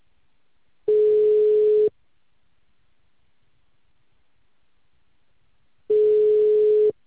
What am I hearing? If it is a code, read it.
Ringback tone (Europe, 425 Hz).